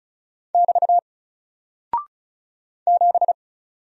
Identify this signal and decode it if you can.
Morse code, heard as a keyed tone, about 36 wpm, 700 Hz. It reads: <BT> 7
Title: Single Character (Letter:Number:Punctuation:Prosign) Alt 35wpm